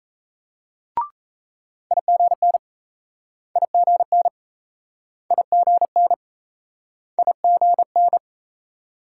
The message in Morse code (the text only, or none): IGN IGN IGN IGN